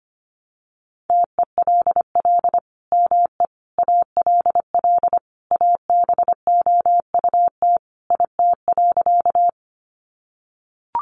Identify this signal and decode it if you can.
Morse code, heard as a keyed tone, about 25 wpm, 700 Hz. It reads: TELL ME ALL ABOUT IT.
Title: Sentences from Top 200 Words 25wpm